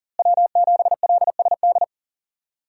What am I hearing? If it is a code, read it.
Morse code, heard as a keyed tone, about 40 wpm, 700 Hz. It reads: W7LSD
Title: USA Call Signs - 5 Characters - Repeated 3x - 40wpm